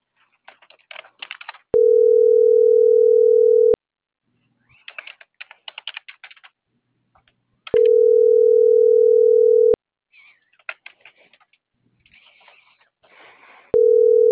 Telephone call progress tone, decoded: ringback tone

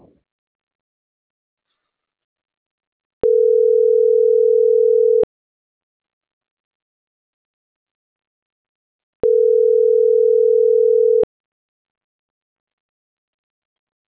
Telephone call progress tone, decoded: ringback tone

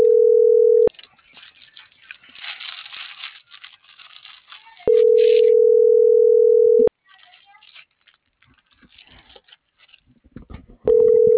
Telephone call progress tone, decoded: ringback tone

